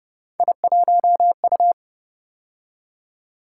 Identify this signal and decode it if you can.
Morse code, heard as a keyed tone, about 30 wpm, 700 Hz. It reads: I1U
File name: Speed-Racing - Three Letter-Number Combinations - 25wpm